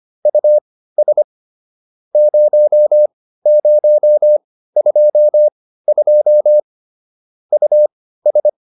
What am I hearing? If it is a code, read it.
Morse code, heard as a keyed tone, about 25 wpm, spaced as for 15 wpm, 590 Hz. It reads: US 0022 US